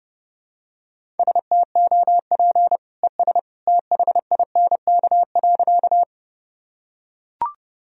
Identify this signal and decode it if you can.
Morse code, heard as a keyed tone, about 30 wpm, 700 Hz. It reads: STOP ES THINK.
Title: Sentences from Top 300 Words and Abbreviations 30wpm